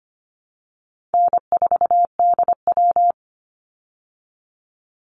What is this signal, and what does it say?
Morse code, heard as a keyed tone, about 25 wpm, 700 Hz. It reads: N4DW
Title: Call Signs Repeated 3x - 25wpm